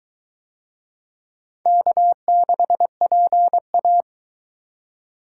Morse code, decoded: K6PA

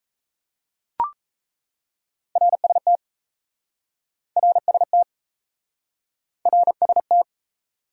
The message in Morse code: RST RST RST

